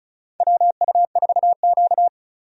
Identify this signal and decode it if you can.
Morse code, heard as a keyed tone, about 34 wpm, 700 Hz. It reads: WU4Q
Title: USA Call Signs - 4 Characters - 35wpm